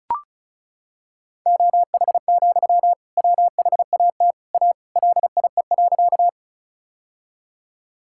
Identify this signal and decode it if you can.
Morse code, heard as a keyed tone, about 36 wpm, 700 Hz. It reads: OH, WHAT A LIE.
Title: Sentences from Top 1600 Words 35wpm